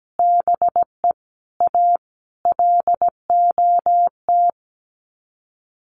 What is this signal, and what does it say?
Morse code, heard as a keyed tone, about 17 wpm, 700 Hz. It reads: BE A LOT
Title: Sets of 5 Words Encore 17wpm